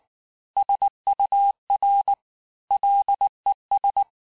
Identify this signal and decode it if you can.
Morse code, heard as a keyed tone, about 19 wpm, 800 Hz. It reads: SUR LES
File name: cours-du-lundi files / cours-1595--FIN-de-QSO-vitesse-18mpm